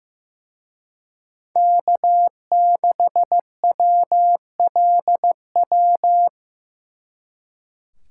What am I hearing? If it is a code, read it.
Morse code, heard as a keyed tone, about 18 wpm, 700 Hz. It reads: K6WLW
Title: Speed Racing - USA Call Signs 15wpm